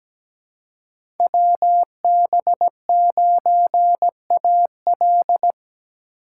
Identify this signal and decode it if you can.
Morse code, heard as a keyed tone, about 17 wpm, 700 Hz. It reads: WB9AL
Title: USA Call Signs - 5 Characters - 3x Character Spacing - 17wpm